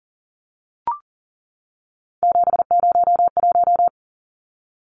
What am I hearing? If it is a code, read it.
Morse code, heard as a keyed tone, about 40 wpm, 700 Hz. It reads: 701
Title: Numbers - 3-Digit 40wpm